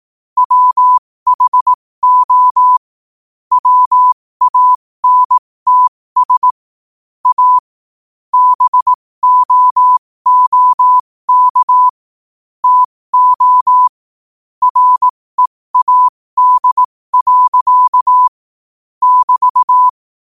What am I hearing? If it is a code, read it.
Morse code, heard as a keyed tone, about 18 wpm, spaced as for 15 wpm, 1000 Hz. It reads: WHO WANTS A BOOK TO READ. <BT>
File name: qotdfiles / 15wpm